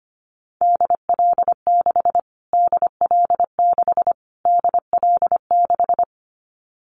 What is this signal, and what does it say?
Morse code, heard as a keyed tone, about 25 wpm, 700 Hz. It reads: DL6 DL6 DL6
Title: Callsign Prefixes for Top 25 DXCC - Repeated 3x - 25wpm